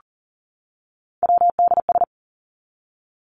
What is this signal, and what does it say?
Morse code, heard as a keyed tone, about 40 wpm, 700 Hz. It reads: WDS
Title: Abbreviations 40wpm